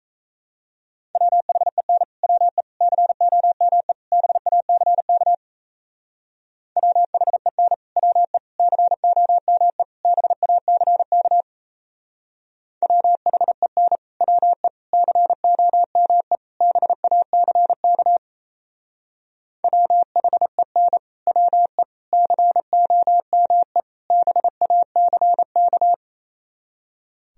Morse code, heard as a keyed tone, about 35 wpm, 700 Hz. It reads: WHEN WE COME BACK WHEN WE COME BACK WHEN WE COME BACK WHEN WE COME BACK
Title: Speed-Racing - Sets of 4 Words 28wpm